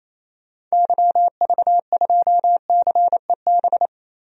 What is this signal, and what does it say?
Morse code, heard as a keyed tone, about 28 wpm, 700 Hz. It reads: YV2CEB
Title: DXCC-100 Call Signs - 6 Characters - 28wpm